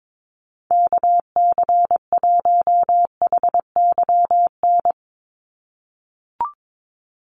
Morse code, heard as a keyed tone, about 22 wpm, 700 Hz. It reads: KC1HYN
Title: USA Call Signs 22wpm